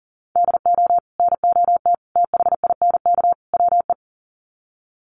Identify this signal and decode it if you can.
Morse code, heard as a keyed tone, about 40 wpm, 700 Hz. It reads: DO NOT THINK WE
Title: Sets of 4 Words 40wpm